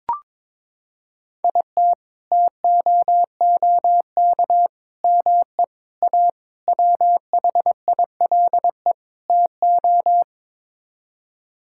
Morse code, heard as a keyed tone, about 21 wpm, 700 Hz. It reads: IT TOOK ME A WHILE TO
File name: practice / 22wpm / Sets of 6 Words - 22wpm